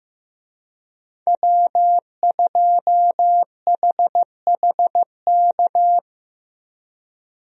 Morse code, heard as a keyed tone, about 15 wpm, 700 Hz. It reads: W2HHK
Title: USA Call Signs - 5 Characters - 15wpm